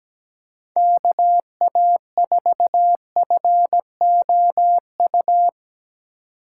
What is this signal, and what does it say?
Morse code, heard as a keyed tone, about 19 wpm, 700 Hz. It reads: KA4FOU
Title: Speed Racing - USA 2x3 Call Signs 17wpm